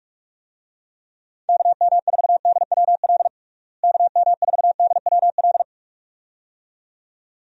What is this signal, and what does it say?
Morse code, heard as a keyed tone, about 45 wpm, 700 Hz. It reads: KM4DWL KM4DWL
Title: USA Call Signs - 6 Characters - Repeated 2x - 45wpm